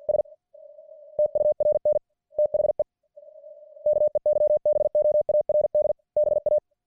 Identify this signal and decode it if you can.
Morse code, heard as a keyed tone, about 50 wpm, 600 Hz. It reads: S TURN THE KEYBOARD BA